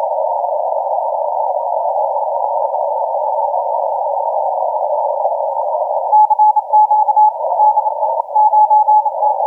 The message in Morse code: none